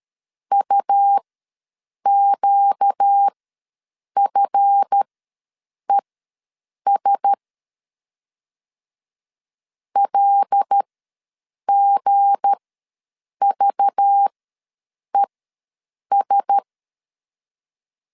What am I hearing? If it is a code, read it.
Morse code, heard as a keyed tone, about 13 wpm, spaced as for 7 wpm, 800 Hz. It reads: UQFES LGVES
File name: LEZIONI / 7 GF9Z / 7 GF ws 12wpm 3140